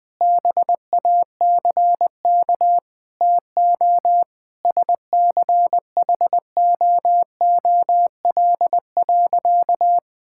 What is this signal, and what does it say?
Morse code, heard as a keyed tone, about 21 wpm, 700 Hz. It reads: BACK TO SCHOOL.
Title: Sentences with 5-Word Phrases - Spoken Beginning - 20wpm